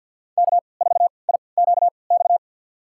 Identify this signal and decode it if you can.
Morse code, heard as a keyed tone, about 48 wpm, spaced as for 28 wpm, 700 Hz. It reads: K4IQX
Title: USA Call Signs - 5 Characters - 3x Character Spacing - 50wpm